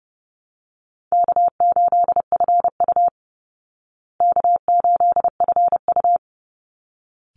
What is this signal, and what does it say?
Morse code, heard as a keyed tone, about 30 wpm, 700 Hz. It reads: K8FU K8FU
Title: Easy Call Signs Repeated 2x - 30wpm